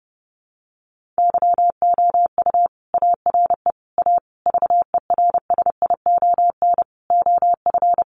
Morse code, heard as a keyed tone, about 30 wpm, 700 Hz. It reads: YOU ARE A VERSION OF
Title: Sentences from Top 1400 Words 30wpm